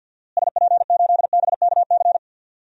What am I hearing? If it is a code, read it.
Morse code, heard as a keyed tone, about 53 wpm, 700 Hz. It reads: SP8BKC
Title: DXCC-100 Call Signs - 6 Characters - Repeated 2x - 50wpm